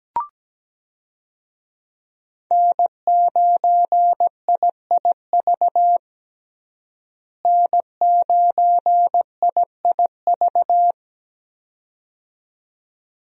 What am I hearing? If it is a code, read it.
Morse code, heard as a keyed tone, about 17 wpm, 700 Hz. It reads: N9IIV N9IIV
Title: USA Call Signs - 5 Characters - Repeated 2x - 17wpm